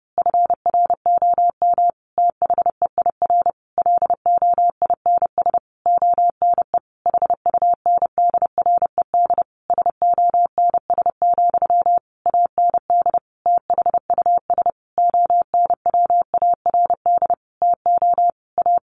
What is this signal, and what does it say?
Morse code, heard as a keyed tone, about 30 wpm, 700 Hz. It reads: FROM THEIR LOINS ONE HUNDRED SONS, AND THUS ONWARD TO A
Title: A Sound Of Thunder 30wpm